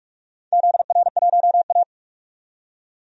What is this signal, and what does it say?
Morse code, heard as a keyed tone, about 45 wpm, 700 Hz. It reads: ZR1A
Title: DXCC-100 Call Signs - 4 Characters - 45wpm